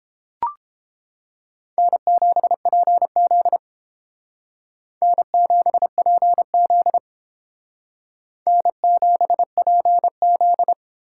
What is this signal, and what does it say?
Morse code, heard as a keyed tone, about 31 wpm, 700 Hz. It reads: N7PZ N7PZ N7PZ